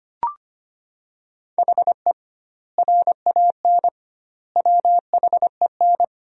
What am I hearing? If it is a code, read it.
Morse code, heard as a keyed tone, about 25 wpm, 700 Hz. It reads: HE RAN WHEN
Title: 2x Word-Spacing - Sentences from Top 500 Words - 25wpm